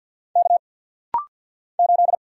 Morse code, heard as a keyed tone, about 49 wpm, 700 Hz. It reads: K 8
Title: Single Letter-Number - Rapid-Fire 50wpm